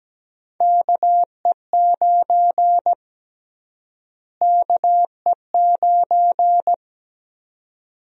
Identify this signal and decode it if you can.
Morse code, heard as a keyed tone, about 17 wpm, 700 Hz. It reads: KE9 KE9
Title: USA Call Sign Prefixes - Repeated 2x - 17wpm